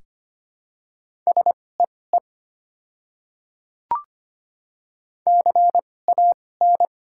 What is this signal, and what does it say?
Morse code, heard as a keyed tone, about 25 wpm, spaced as for 18 wpm, 700 Hz. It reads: SEE CAN
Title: 2x Character Spacing - Top 100 Words - 25wpm